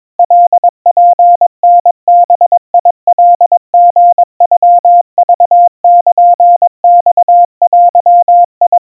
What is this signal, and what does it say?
Morse code (Japanese Wabun code), heard as a keyed tone, about 22 wpm, 700 Hz. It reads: カツタバカリノクルマデ
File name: 80 cpm 700 Hz Crispy